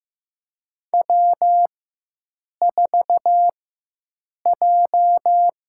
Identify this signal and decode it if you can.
Morse code, heard as a keyed tone, about 15 wpm, spaced as for 7 wpm, 700 Hz. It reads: W4J
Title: USA Call Signs - 5 Characters - 4x Character Spacing - 15wpm